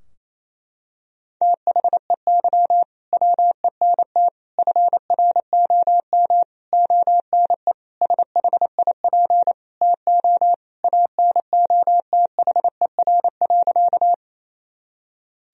Morse code, heard as a keyed tone, about 28 wpm, 700 Hz. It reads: THEY WENT FROM ONE SHIP TO ANOTHER.
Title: Sentences from Top 1000 Words - Review Part 2 - 28wpm